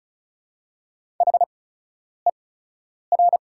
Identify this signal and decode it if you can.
Morse code, heard as a keyed tone, about 36 wpm, spaced as for 10 wpm, 690 Hz. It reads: HER
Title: 8x Character Spacing - Top 100 Words - 35wpm